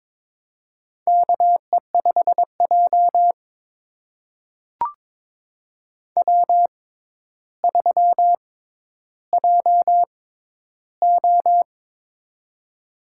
Morse code, heard as a keyed tone, about 22 wpm, spaced as for 11 wpm, 700 Hz. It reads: KE5J W3JO
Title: USA Call Signs - 4 Characters - 6x Character Spacing - 22wpm